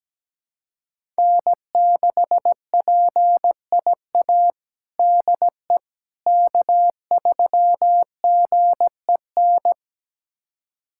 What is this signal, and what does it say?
Morse code, heard as a keyed tone, about 17 wpm, 700 Hz. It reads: N6PIA DE K3GEN
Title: USA Callsign DE Callsign - 5 Characters - 17wpm